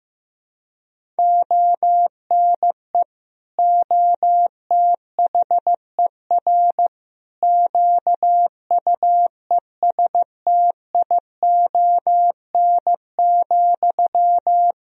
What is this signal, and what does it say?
Morse code, heard as a keyed tone, about 15 wpm, 700 Hz. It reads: ONE OTHER QUESTION,
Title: Sentences from Top 400 Words and Abbreviations 15wpm